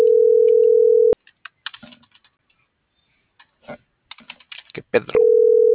Ringback tone (North America), 440 + 480 Hz.